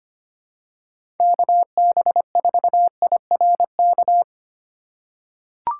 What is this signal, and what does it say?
Morse code, heard as a keyed tone, about 25 wpm, 700 Hz. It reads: KB4IRK
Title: USA Call Signs - 6 Characters - 2x Character Spacing - 25wpm